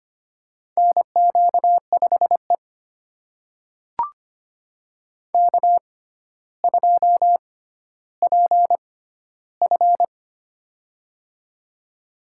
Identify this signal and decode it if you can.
Morse code, heard as a keyed tone, about 25 wpm, spaced as for 13 wpm, 700 Hz. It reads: NQ5E K2PF
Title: USA Call Signs - 4 Characters - 6x Character Spacing - 25wpm